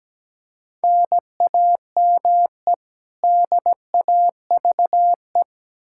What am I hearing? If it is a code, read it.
Morse code, heard as a keyed tone, about 17 wpm, 700 Hz. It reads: NAME DAVE
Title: Practice QSO - Exchange 1 - Beginning - 17wpm